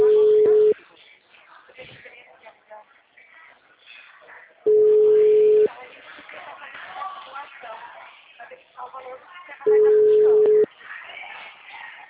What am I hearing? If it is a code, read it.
Ringback tone (Europe, 425 Hz), with speech.